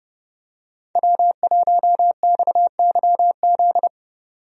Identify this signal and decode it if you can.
Morse code, heard as a keyed tone, about 32 wpm, 700 Hz. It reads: W1XYZ
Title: USA Call Signs - 5 Characters - 30wpm